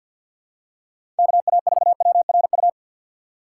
Morse code, heard as a keyed tone, about 53 wpm, 700 Hz. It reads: KA4WRU
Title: USA Call Signs - 6 Characters - 6x Character Spacing - 50wpm